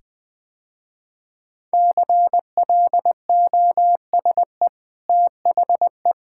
Morse code, heard as a keyed tone, about 20 wpm, 700 Hz. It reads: CLOSE THE
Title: Easy Sentences with Hints - 20wpm